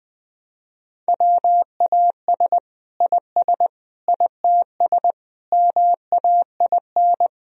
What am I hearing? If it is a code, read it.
Morse code, heard as a keyed tone, about 21 wpm, 700 Hz. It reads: WAS IS ITS MAIN
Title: Sentences from Top 1000 Words - Review Part 2 - 20wpm